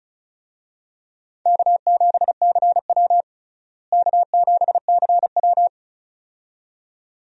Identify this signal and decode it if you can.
Morse code, heard as a keyed tone, about 36 wpm, 700 Hz. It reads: K7CW K7CW
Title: USA Call Signs - 4 Characters - Repeated 2x - 35wpm